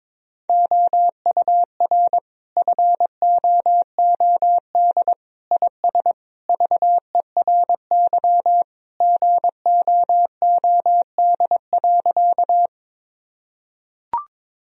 Morse code, heard as a keyed tone, about 21 wpm, 700 Hz. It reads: OUR FOOD IS VERY GOOD.
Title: Sentences from Top 1000 Words - Review Part 1 - 22wpm